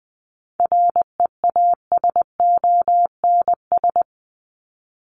Morse code, heard as a keyed tone, about 20 wpm, 700 Hz. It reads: REASONS
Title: ES:S Suffix - Top 100 Words with Base Words 20wpm